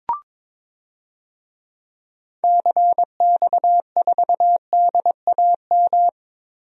Morse code, heard as a keyed tone, about 22 wpm, 700 Hz. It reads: CX4DAM
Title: DXCC-100 Call Signs - 6 Characters - Repeated 2x - 22wpm